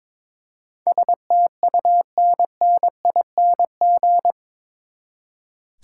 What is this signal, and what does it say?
Morse code, heard as a keyed tone, about 22 wpm, 700 Hz. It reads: STUNNING
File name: practice / 22wpm / Eight Letter Words - 22wpm